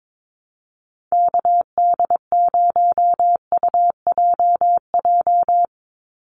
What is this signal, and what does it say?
Morse code, heard as a keyed tone, about 22 wpm, 700 Hz. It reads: KD0UJJ
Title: USA Call Signs 22wpm